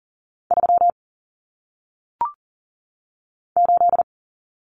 Morse code, heard as a keyed tone, about 40 wpm, 700 Hz. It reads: 3 8
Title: Common Cut Numbers - 1-Digit 40wpm